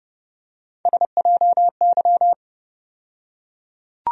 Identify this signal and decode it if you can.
Morse code, heard as a keyed tone, about 31 wpm, 700 Hz. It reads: SJY
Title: USA Call Sign Postfixes - 3 Letters 30wpm